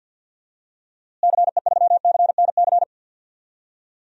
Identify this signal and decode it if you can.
Morse code, heard as a keyed tone, about 47 wpm, 700 Hz. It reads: CE3CNC
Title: DXCC-100 Call Signs - 6 Characters - Repeated 3x - 50wpm